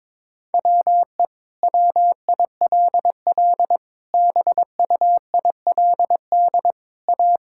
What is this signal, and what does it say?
Morse code, heard as a keyed tone, about 22 wpm, 700 Hz. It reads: WE WILL BUILD A
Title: Sentences from Top 1000 Words - Review Part 6 - 22wpm